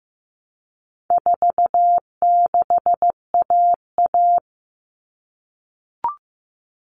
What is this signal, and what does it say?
Morse code, heard as a keyed tone, about 15 wpm, 700 Hz. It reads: 46AA
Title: All Cut Numbers - 4-Digit 15wpm